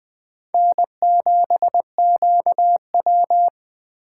Morse code, heard as a keyed tone, about 20 wpm, 700 Hz. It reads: N7QW
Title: Speed Racing - USA 1x2 Call Signs 17wpm